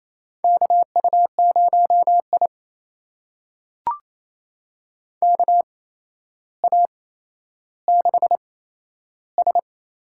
Morse code, heard as a keyed tone, about 28 wpm, spaced as for 12 wpm, 700 Hz. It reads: KU0I KA6S
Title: USA Call Signs - 4 Characters - 8x Character Spacing - 28wpm